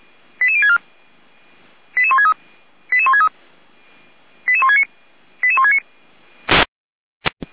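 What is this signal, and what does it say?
Five-tone selective call (ZVEI), heard as quick runs of five tones, tones of about 80 ms. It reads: 88064 88163 88163 88169 88169
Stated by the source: ZVEI five-tone selective call received on an SDR.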